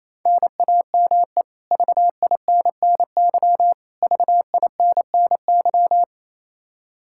Morse code, heard as a keyed tone, about 28 wpm, 700 Hz. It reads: NAME VINNY VINNY
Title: K1USN SST QSO Practice 28wpm